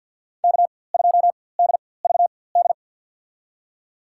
Morse code, heard as a keyed tone, about 53 wpm, spaced as for 24 wpm, 700 Hz. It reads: K2DVD
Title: USA Call Signs - 5 Characters - 4x Character Spacing - 50wpm